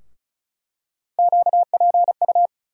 Morse code, heard as a keyed tone, about 36 wpm, 700 Hz. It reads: QPU
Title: USA Call Sign Postfixes - 3 Letters 35wpm